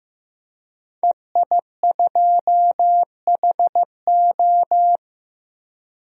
Morse code, heard as a keyed tone, about 16 wpm, 700 Hz. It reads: EI2HO
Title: DXCC-100 Call Signs - 5 Characters - 15wpm